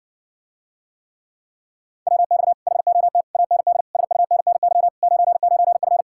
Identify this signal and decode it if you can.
Morse code, heard as a keyed tone, about 60 wpm, 700 Hz. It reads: WX HOT AND SUNNY 99F